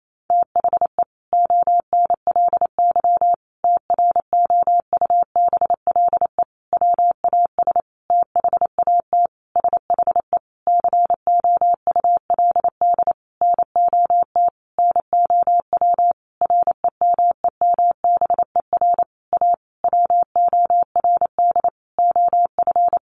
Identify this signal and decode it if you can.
Morse code, heard as a keyed tone, about 28 wpm, 700 Hz. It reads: THE ONLY TROUBLE WAS THAT SHE COULD NOT NOW REMEMBER A WORD OF
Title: Sentences from Top 1200 Words 28wpm